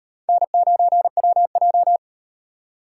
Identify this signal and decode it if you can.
Morse code, heard as a keyed tone, about 38 wpm, 700 Hz. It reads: N9WJ